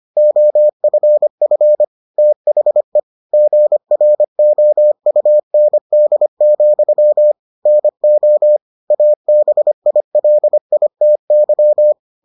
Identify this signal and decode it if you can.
Morse code, heard as a keyed tone, about 25 wpm, 600 Hz. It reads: OFF THE GROUND, NO ABILITY